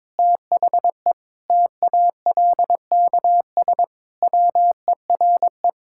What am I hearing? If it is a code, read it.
Morse code, heard as a keyed tone, about 22 wpm, 700 Hz. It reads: THE TALKS WERE
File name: practice / 22wpm / Sentences with 8-Word Phrases - 22wpm